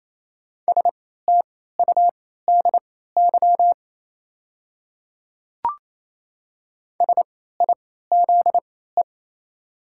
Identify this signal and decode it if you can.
Morse code, heard as a keyed tone, about 28 wpm, spaced as for 16 wpm, 700 Hz. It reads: STUDY SIZE
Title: 3x Character Spacing - Top 500 Words - 28wpm